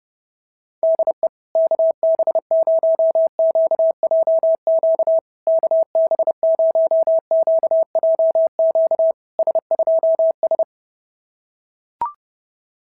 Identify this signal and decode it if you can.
Morse code, heard as a keyed tone, about 32 wpm, 650 Hz. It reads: DE KB0QJQ KB0QJQ S2S
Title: SOTA QSO Practice 30wpm